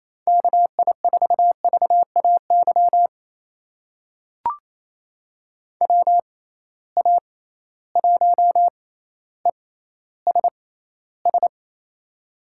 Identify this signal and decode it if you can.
Morse code, heard as a keyed tone, about 28 wpm, spaced as for 14 wpm, 700 Hz. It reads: KI4VAY WA1ESS
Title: USA Call Signs - 6 Characters - 6x Character Spacing - 28wpm